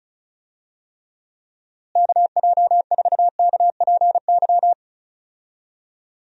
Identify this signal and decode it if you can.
Morse code, heard as a keyed tone, about 34 wpm, 700 Hz. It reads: KJ4KPY
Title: USA Call Signs - 6 Characters - Repeated 2x - 35wpm